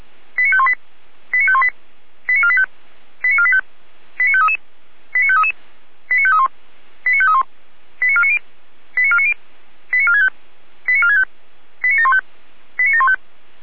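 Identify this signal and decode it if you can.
Five-tone selective call (ZVEI), heard as quick runs of five tones, tones of about 70 ms. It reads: 89518 89518 89485 89485 89533 89533 89531 89531 89490 89490 89465 89465 89715 89715
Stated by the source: Received by SDR; ZVEI five-tone call.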